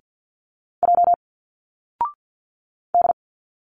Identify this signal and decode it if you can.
Morse code, heard as a keyed tone, about 49 wpm, 700 Hz. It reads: J D
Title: Single Letters 50wpm